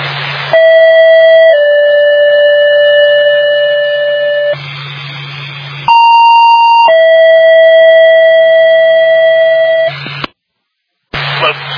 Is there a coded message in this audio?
Yes: two-tone page from 0.5 to 9.9 s, heard as a one-second tone, then a three-second tone, about 650 then 580 Hz, about 930 then 650 Hz, with speech.